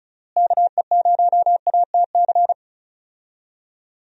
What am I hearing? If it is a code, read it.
Morse code, heard as a keyed tone, about 36 wpm, 700 Hz. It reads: KE0ATC
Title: USA Call Signs - 6 Characters - Repeated 3x - 35wpm